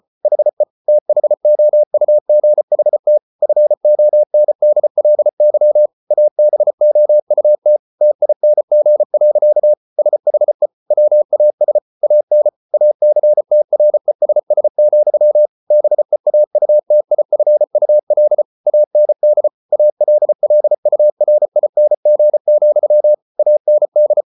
Morse code, heard as a keyed tone, about 34 wpm, 600 Hz. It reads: HE THOUGHT FONDLY ABOUT TING. SHE WAS AN ACTRESS, BEAUTIFUL AND ALLURING, AND